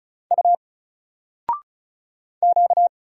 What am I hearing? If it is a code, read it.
Morse code, heard as a keyed tone, about 35 wpm, 700 Hz. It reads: U Q
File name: Single-Letter-Number-35wpm